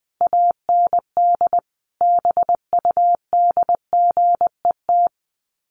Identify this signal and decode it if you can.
Morse code, heard as a keyed tone, about 20 wpm, 700 Hz. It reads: AND BUDGET
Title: Sets of 5 Words Encore2 20wpm